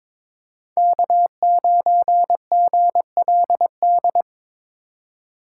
Morse code, heard as a keyed tone, about 26 wpm, 700 Hz. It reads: K9GLD
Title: Speed Racing - USA 1x3 Call Signs 22wpm